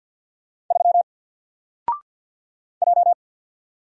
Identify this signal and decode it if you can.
Morse code, heard as a keyed tone, about 48 wpm, 700 Hz. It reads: 3 J
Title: Single Letter-Number Alt 50wpm